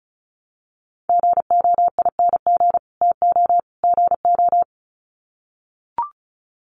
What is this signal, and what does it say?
Morse code, heard as a keyed tone, about 34 wpm, 700 Hz. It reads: GOING TO GO
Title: Sets of 3 Words 35wpm